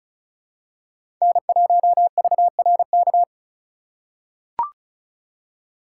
Morse code, heard as a keyed tone, about 36 wpm, 700 Hz. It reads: N1VRK
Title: USA Call Signs - 5 Characters - 4x Character Spacing - 35wpm